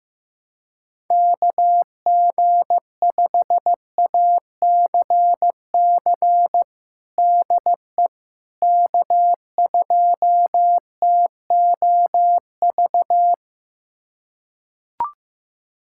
Morse code, heard as a keyed tone, about 16 wpm, 700 Hz. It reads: KG5ACC DE K2TOV
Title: USA Callsign DE Callsign 15wpm